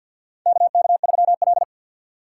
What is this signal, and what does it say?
Morse code, heard as a keyed tone, about 52 wpm, 700 Hz. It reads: KK3L
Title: USA Call Signs - 6x Character Spacing - 50wpm